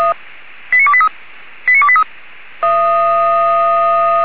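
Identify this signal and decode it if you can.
Five-tone selective call (ZVEI), heard as quick runs of five tones, tones of about 70 ms. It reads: 89283 89283
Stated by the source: ZVEI five-tone selective call received by SDR.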